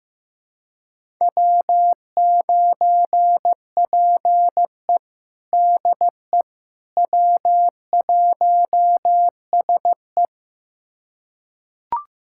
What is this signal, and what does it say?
Morse code, heard as a keyed tone, about 15 wpm, 700 Hz. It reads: W9PE DE W1SE
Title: USA Callsign DE Callsign - 4 Characters - 15wpm